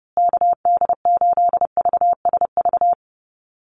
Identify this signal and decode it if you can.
Morse code, heard as a keyed tone, about 30 wpm, 700 Hz. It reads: KD8VSV
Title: USA Call Signs Repeated 3x 30wpm